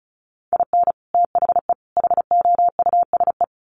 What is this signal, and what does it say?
Morse code, heard as a keyed tone, about 34 wpm, 700 Hz. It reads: IN THE HOUSE
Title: Sets of 3 Words 35wpm